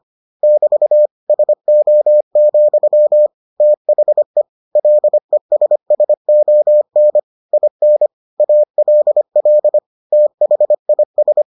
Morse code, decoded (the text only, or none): <BT> SO, THE LESSON IN ALL THIS